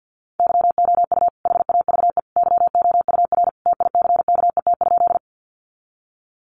Morse code, heard as a keyed tone, about 50 wpm, 700 Hz. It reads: YOU HAVE YOUR TICKET?
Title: Sentences from Top 1400 Words 50wpm